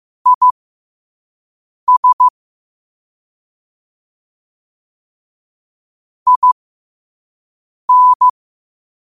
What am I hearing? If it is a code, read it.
Morse code, heard as a keyed tone, about 15 wpm, spaced as for 5 wpm, 1000 Hz. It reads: IS IN
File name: qotdfiles / 5wpm